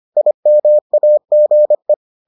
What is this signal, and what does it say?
Morse code, heard as a keyed tone, about 25 wpm, 600 Hz. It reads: IMAGE